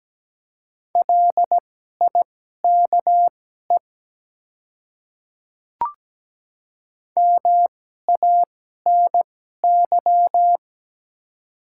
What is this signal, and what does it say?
Morse code, heard as a keyed tone, about 17 wpm, spaced as for 12 wpm, 700 Hz. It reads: LIKE MANY
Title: 2x Character Spacing - Top 100 Words - 17wpm